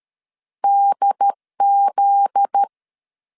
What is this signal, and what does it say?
Morse code, heard as a keyed tone, about 13 wpm, 800 Hz. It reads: DZ
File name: CW / DET / 1 9 1500C S C 12wpm 3811